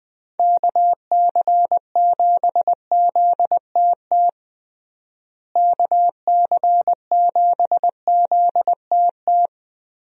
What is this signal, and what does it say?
Morse code, heard as a keyed tone, about 19 wpm, 700 Hz. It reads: KC7ZTT KC7ZTT
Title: USA Call Signs - 6 Characters - Repeated 3x - 20wpm